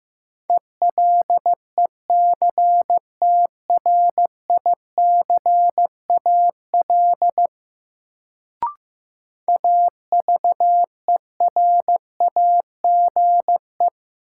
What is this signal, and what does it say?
Morse code, heard as a keyed tone, about 15 wpm, 700 Hz. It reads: ELECTRICAL AVERAGE
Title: Sentences from Top 1000 Words - Review Part 6 - New Word Drill - 15wpm